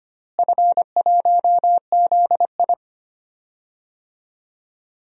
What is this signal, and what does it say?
Morse code, heard as a keyed tone, about 25 wpm, 700 Hz. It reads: F1ZI
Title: DXCC-100 Call Signs - 4 Characters - Repeated 3x - 25wpm